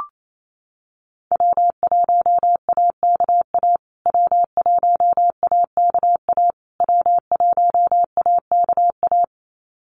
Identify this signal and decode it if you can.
Morse code, heard as a keyed tone, about 28 wpm, 700 Hz. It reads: W1AKA W1AKA W1AKA
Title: USA Call Signs Repeated 3x 28wpm